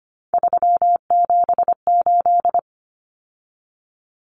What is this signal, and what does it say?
Morse code, heard as a keyed tone, about 25 wpm, 700 Hz. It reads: 378
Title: Common Cut Numbers - 3-Digit 25wpm